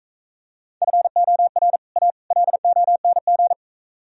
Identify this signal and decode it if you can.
Morse code, heard as a keyed tone, about 43 wpm, 700 Hz. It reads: FOR A LONG